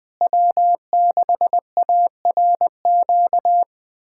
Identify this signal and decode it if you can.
Morse code, heard as a keyed tone, about 20 wpm, 700 Hz. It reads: W6ARQ
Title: USA Callsign DE Callsign - 5 Characters - 20wpm